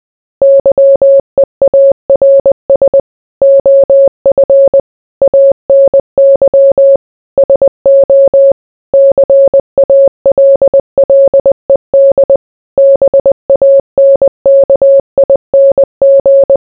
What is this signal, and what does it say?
Morse code, heard as a keyed tone, about 20 wpm, 550 Hz. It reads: YEARS OF ANY SO CALLED BANKING